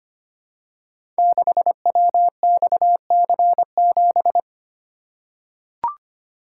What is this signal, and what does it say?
Morse code, heard as a keyed tone, about 25 wpm, 700 Hz. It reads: 6WXC7
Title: Code Groups - 5 Letter:Numbers - 25wpm